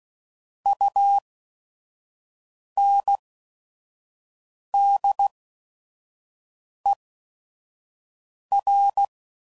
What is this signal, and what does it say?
Morse code, heard as a keyed tone, about 16 wpm, spaced as for 5 wpm, 800 Hz. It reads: UNDER
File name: Prefixes / By First Letter / Us